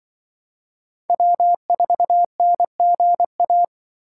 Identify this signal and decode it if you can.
Morse code, heard as a keyed tone, about 24 wpm, 700 Hz. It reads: W4NGA
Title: Speed Racing - USA 1x3 Call Signs 20wpm